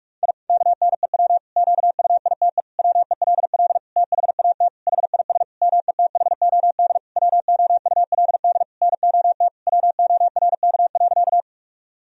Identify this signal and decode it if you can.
Morse code, heard as a keyed tone, about 46 wpm, 700 Hz. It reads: I KNEW QUITE WELL THAT HIS METHOD WOULD NOT WORK.
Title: Sentences from Top 1000 Words - Review Part 3 - 45wpm